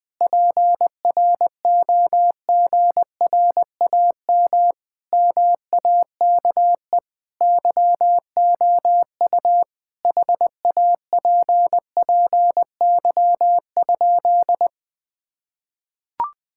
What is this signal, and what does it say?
Morse code, heard as a keyed tone, about 20 wpm, 700 Hz. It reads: PROGRAM MAKE YOU HAPPY?
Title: Sentences from Top 1000 Words - Review Part 5 - 20wpm